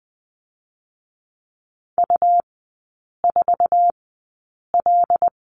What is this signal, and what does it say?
Morse code, heard as a keyed tone, about 20 wpm, spaced as for 8 wpm, 700 Hz. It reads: U4L
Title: 127 Lesson 21e - Number 6 - Three Characters 20wpm